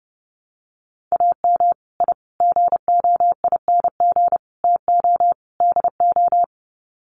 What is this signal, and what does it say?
Morse code, heard as a keyed tone, about 30 wpm, 700 Hz. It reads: AM I GOING TO DO
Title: Sets of 5 Words Encore 30wpm